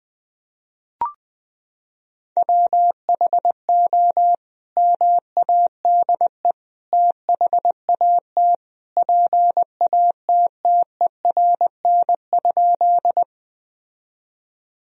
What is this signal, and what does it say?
Morse code, heard as a keyed tone, about 20 wpm, 700 Hz. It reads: WHO MADE THAT PATTERN?
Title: Sentences from Top 1000 Words - Review Part 2 - 20wpm